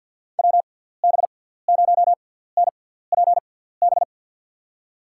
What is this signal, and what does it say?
Morse code, heard as a keyed tone, about 52 wpm, spaced as for 17 wpm, 700 Hz. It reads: WB0NPB
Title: USA Call Signs - 6 Characters - 6x Character Spacing - 50wpm